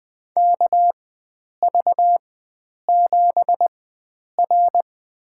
Morse code, heard as a keyed tone, about 20 wpm, spaced as for 9 wpm, 700 Hz. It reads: KV7R
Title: USA Call Signs - 4 Characters - 4x Character Spacing - 20wpm